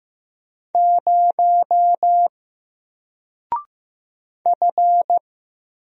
Morse code, heard as a keyed tone, about 15 wpm, 700 Hz. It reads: 0 F
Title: Single Character (Letter:Number:Punctuation:Prosign) Alt 15wpm